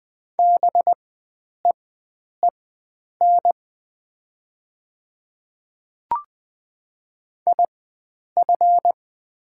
Morse code, heard as a keyed tone, about 20 wpm, spaced as for 9 wpm, 700 Hz. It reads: BEEN IF
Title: 4x Character Spacing - Top 100 Words - 20wpm